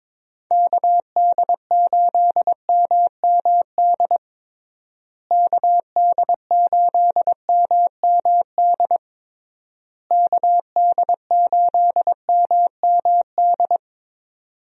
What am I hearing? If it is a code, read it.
Morse code, heard as a keyed tone, about 22 wpm, 700 Hz. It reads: KD8MMD KD8MMD KD8MMD
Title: USA Call Signs - 6 Characters - Repeated 3x - 22wpm